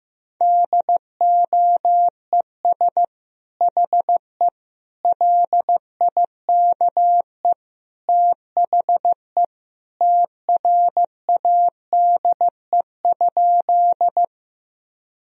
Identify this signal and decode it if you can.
Morse code, heard as a keyed tone, about 15 wpm, 700 Hz. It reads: DOES HE LIKE THE TRADE?
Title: Sentences from Top 1000 Words - Review Part 4 - 15wpm